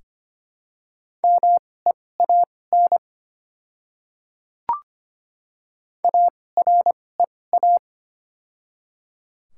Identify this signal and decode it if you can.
Morse code, heard as a keyed tone, about 25 wpm, spaced as for 18 wpm, 700 Hz. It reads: MEAN AREA
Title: 2x Character Spacing - Top 500 Words - 25wpm